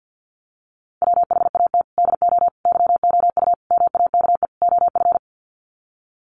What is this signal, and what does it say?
Morse code, heard as a keyed tone, about 50 wpm, 700 Hz. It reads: WHAT DO YOU MAKE OF
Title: Sets of 5 Words 50wpm